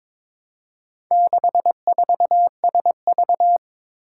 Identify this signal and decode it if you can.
Morse code, heard as a keyed tone, about 22 wpm, 700 Hz. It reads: 64SV
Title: Four Challenging Letter-Number Combinations 22wpm